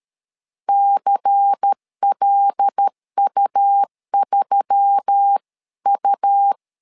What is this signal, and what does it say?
Morse code, heard as a keyed tone, about 13 wpm, 800 Hz. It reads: CLU3U
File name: LEZIONI / 5 LBVC / 5 LBVC 12wpm 1955